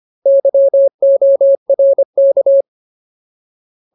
Morse code, heard as a keyed tone, about 25 wpm, 550 Hz. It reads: YORK
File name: Ultimate Word List / 25WPM Ultimate